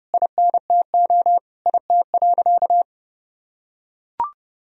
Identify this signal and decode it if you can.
Morse code, heard as a keyed tone, about 30 wpm, 700 Hz. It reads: INTO IT.
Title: Sentences from Top 1000 Words - Review Part 1 - 30wpm